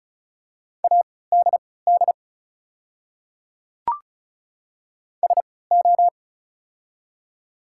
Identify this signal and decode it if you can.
Morse code, heard as a keyed tone, about 34 wpm, spaced as for 20 wpm, 700 Hz. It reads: ADD SO
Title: 3x Character Spacing - Top 200 Words - 35wpm